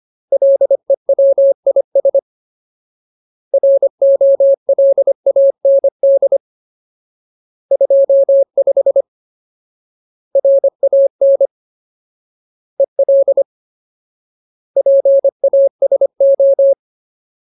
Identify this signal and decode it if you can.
Morse code, heard as a keyed tone, about 25 wpm, 550 Hz. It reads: LEWIS ROLAND 25 RAN EL PASO
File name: Ultimate Word List / 25WPM Ultimate